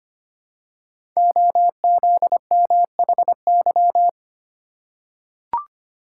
Morse code, heard as a keyed tone, about 25 wpm, 700 Hz. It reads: OZMHY
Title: Code Groups - 5 Letters - 25wpm